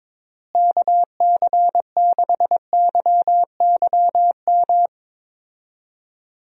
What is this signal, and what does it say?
Morse code, heard as a keyed tone, about 22 wpm, 700 Hz. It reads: KC6YYM
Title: USA Call Signs - 6 Characters - Repeated 2x - 22wpm